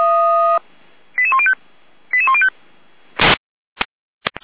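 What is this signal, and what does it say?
Five-tone selective call (ZVEI), heard as quick runs of five tones, tones of about 70 ms. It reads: 88105 88105
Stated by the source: ZVEI five-tone selective call received on an SDR, from the feuerwehr-orlamuende.org fire brigade audio server.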